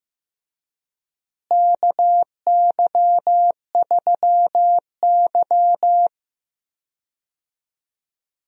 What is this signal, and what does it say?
Morse code, heard as a keyed tone, about 15 wpm, 700 Hz. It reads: KY3Y
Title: USA Call Signs - 4 Characters - Repeated 2x - 15wpm